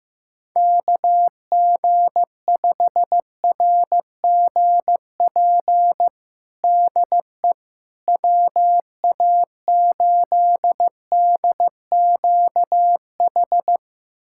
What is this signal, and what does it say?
Morse code, heard as a keyed tone, about 15 wpm, 700 Hz. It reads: KG5RGP DE WA8DQH
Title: USA Callsign DE Callsign 15wpm